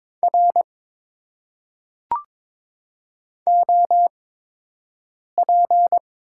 Morse code, heard as a keyed tone, about 22 wpm, spaced as for 6 wpm, 700 Hz. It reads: R OP
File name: Top 90 QSO Elements - 8x Character Spacing - 22wpm